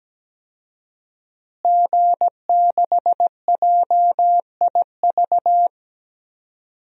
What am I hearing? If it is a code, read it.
Morse code, heard as a keyed tone, about 17 wpm, 700 Hz. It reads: G6JIV
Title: DXCC-100 Call Signs - 5 Characters - Repeated 3x - 17wpm